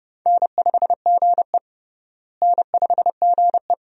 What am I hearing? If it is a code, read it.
Morse code, heard as a keyed tone, about 32 wpm, 700 Hz. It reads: N5GE N5GE
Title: USA Call Signs - 4 Characters - Repeated 2x - 30wpm